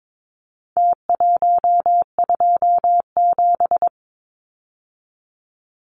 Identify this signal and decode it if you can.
Morse code, heard as a keyed tone, about 22 wpm, 700 Hz. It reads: T127
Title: Numbers - Contest Style - 4-Digit 22wpm